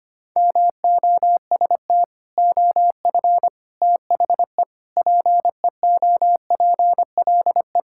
Morse code, heard as a keyed tone, about 25 wpm, 700 Hz. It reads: MOST OF THE PEOPLE
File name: practice / 25wpm / Sentences with 3-Word Phrases - 25wpm